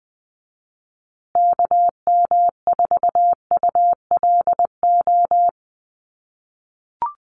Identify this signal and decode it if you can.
Morse code, heard as a keyed tone, about 20 wpm, 700 Hz. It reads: KM4ULO
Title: USA Call Signs Repeated 3x 20wpm